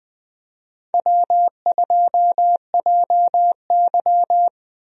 Morse code, heard as a keyed tone, about 20 wpm, 700 Hz. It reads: W2JY
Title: USA Call Signs - 4 Characters - Repeated 3x - 20wpm